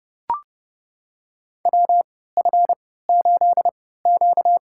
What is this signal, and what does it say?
Morse code, heard as a keyed tone, about 30 wpm, spaced as for 17 wpm, 700 Hz. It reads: WF8Q
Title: USA Call Signs - 4 Characters - 3x Character Spacing - 30wpm